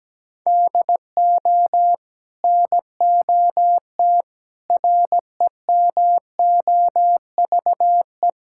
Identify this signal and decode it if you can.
Morse code, heard as a keyed tone, about 17 wpm, 700 Hz. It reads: DO NOT REMOVE
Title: Sentences from Top 1000 Words - Review Part 5 - 17wpm